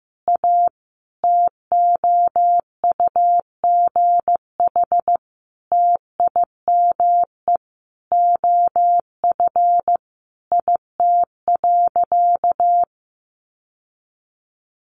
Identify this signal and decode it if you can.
Morse code, heard as a keyed tone, about 15 wpm, 700 Hz. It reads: A TOUGH TIME OF IT.
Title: Sentences from Top 1200 Words 15wpm